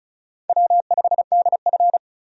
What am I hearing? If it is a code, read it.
Morse code, heard as a keyed tone, about 36 wpm, 700 Hz. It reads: W5DF
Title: USA Call Signs - 4 Characters - 35wpm